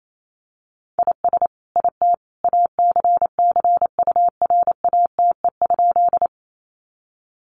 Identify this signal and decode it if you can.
Morse code, heard as a keyed tone, about 28 wpm, 700 Hz. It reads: IS IT ACCURATE?
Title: Sentences from Top 1200 Words 28wpm